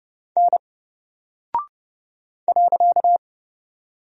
Morse code, heard as a keyed tone, about 30 wpm, 700 Hz. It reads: N .
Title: Single Character (Letter:Number:Punctuation:Prosign) Alt 30wpm